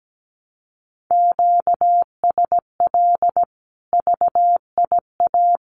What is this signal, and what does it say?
Morse code, heard as a keyed tone, about 17 wpm, 700 Hz. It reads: QSL VIA
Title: Common QSO Phrases 17wpm